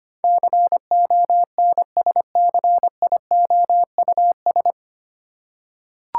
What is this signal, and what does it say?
Morse code, heard as a keyed tone, about 25 wpm, 690 Hz. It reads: CONSCIOUS
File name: practice / 25wpm / 9 Letter Words - First 3 Letters Given - 25wpm